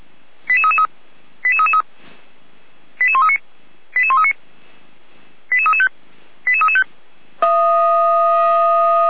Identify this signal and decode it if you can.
Five-tone selective call (ZVEI), heard as quick runs of five tones, tones of about 70 ms. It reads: 88333 88333 88139 88139 88335 88335
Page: audio.f85360.feuerwehr-orlamuende.org